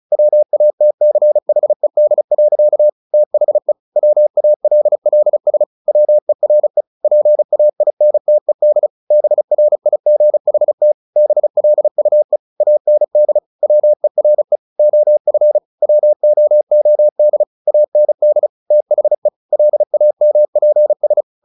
Morse code, heard as a keyed tone, about 34 wpm, 600 Hz. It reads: WATCHED. THE WALLS WERE PAINTED BRIGHT BLUE AND WERE OF WOOD AND THE LAMPS
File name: the old man and the sea 2 35wpm